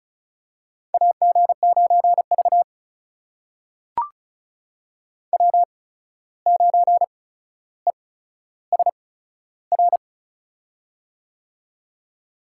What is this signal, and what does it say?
Morse code, heard as a keyed tone, about 36 wpm, spaced as for 14 wpm, 700 Hz. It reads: AG9V W9ESR
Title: USA Call Signs - 8x Character Spacing - 35wpm